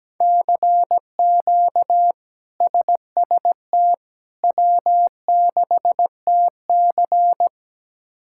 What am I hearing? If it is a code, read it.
Morse code, heard as a keyed tone, about 17 wpm, 700 Hz. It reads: CQ SST W6TC
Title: K1USN SST QSO Practice - 17wpm